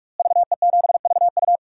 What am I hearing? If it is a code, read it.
Morse code, heard as a keyed tone, about 44 wpm, 700 Hz. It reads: VE7VU
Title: DXCC-100 Call Signs - 5 Characters - Repeated 3x - 45wpm